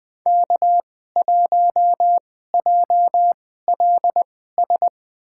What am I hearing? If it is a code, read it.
Morse code, heard as a keyed tone, about 20 wpm, spaced as for 15 wpm, 700 Hz. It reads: K1JLS
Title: USA Call Signs - 5 Characters - 2x Character Spacing - 20wpm